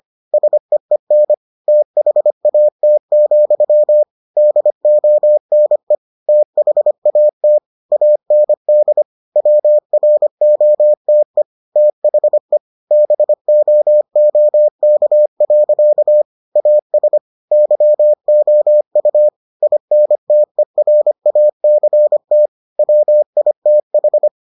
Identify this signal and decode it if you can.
Morse code, heard as a keyed tone, about 25 wpm, 600 Hz. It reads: SEEN THAT, DONE THAT AND WROTE THE BOOK. AS YOU INTERACT WITH